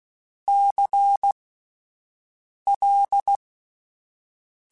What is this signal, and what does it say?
Morse code, heard as a keyed tone, about 16 wpm, spaced as for 5 wpm, 800 Hz. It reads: CL
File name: Alphabetical / Alphabetical mp3s / Cs @5